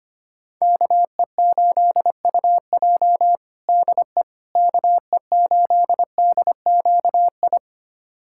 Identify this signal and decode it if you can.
Morse code, heard as a keyed tone, about 25 wpm, 700 Hz. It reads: KE8UJ DE KE8DQI
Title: USA Callsign DE Callsign 25wpm